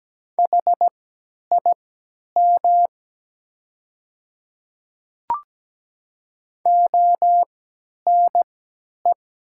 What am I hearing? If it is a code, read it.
Morse code, heard as a keyed tone, about 17 wpm, spaced as for 10 wpm, 700 Hz. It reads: HIM ONE
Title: 3x Character Spacing - Top 300 Words - 17wpm